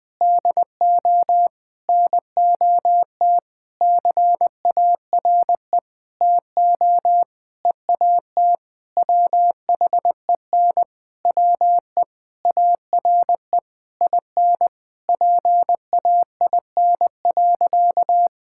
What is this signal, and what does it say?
Morse code, heard as a keyed tone, about 21 wpm, 700 Hz. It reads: DO NOT CARE TO EAT WHEN WE ARE IN PAIN.
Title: Sentences from Top 1600 Words 20wpm